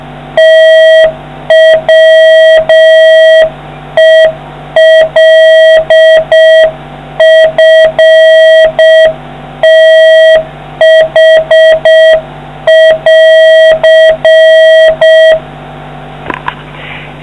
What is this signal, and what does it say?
Morse code, heard as a keyed tone, about 6 wpm, 1950 Hz, with noise. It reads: TWELFTH<AR>